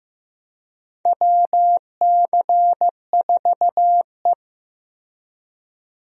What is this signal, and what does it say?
Morse code, heard as a keyed tone, about 15 wpm, 700 Hz. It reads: WC4E
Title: USA Call Signs - 4 Characters - 15wpm